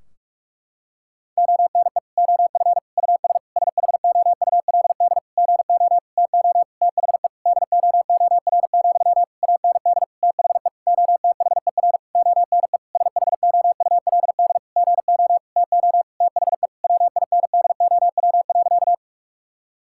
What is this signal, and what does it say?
Morse code, heard as a keyed tone, about 46 wpm, 700 Hz. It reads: ONE OF US SHOULD GO TO THE DOOR, AND THE OTHER ONE SHOULD GO TO THE WINDOW.
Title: Sentences from Top 1600 Words 45wpm